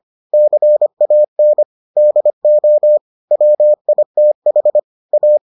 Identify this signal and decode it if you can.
Morse code, heard as a keyed tone, about 25 wpm, 600 Hz. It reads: CAN DO WITH A